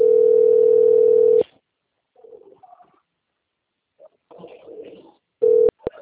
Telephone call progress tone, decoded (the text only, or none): ringback tone